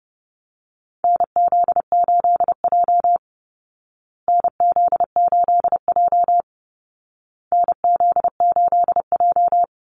Morse code, heard as a keyed tone, about 30 wpm, 700 Hz. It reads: NZ8J NZ8J NZ8J
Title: Call Signs Repeated 2x - 30wpm